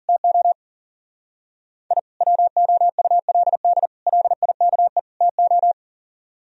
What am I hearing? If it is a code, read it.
Morse code, heard as a keyed tone, about 52 wpm, 700 Hz. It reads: TO I WOULD LIKE TO